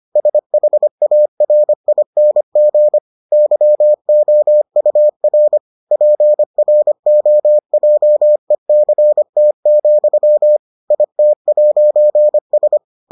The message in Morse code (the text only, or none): SHARING YOUR PROJECT, IT'S